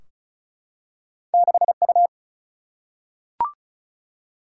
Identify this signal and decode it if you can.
Morse code, heard as a keyed tone, about 36 wpm, 700 Hz. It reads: 6U
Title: Single-Letter Single-Number 35wpm